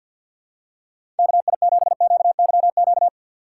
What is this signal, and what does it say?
Morse code, heard as a keyed tone, about 52 wpm, 700 Hz. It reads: KI7QYQ
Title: USA Call Signs - 3x Character Spacing - 50wpm